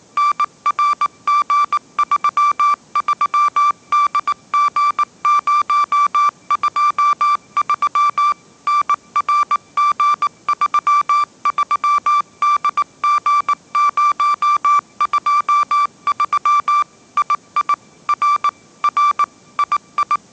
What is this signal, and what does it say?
Morse code, heard as a keyed tone, about 19 wpm, 1170 Hz. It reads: NRG33DG023 NRG33DG023 II R R II
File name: 20wpm